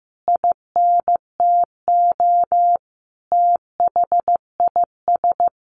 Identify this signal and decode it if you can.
Morse code, heard as a keyed tone, about 15 wpm, 700 Hz. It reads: INTO THIS
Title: Sentences from Top 1400 Words 15wpm